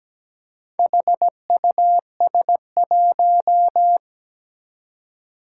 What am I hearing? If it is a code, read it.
Morse code, heard as a keyed tone, about 17 wpm, 700 Hz. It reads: HUS1
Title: Four Challenging Letter-Number Combinations 17wpm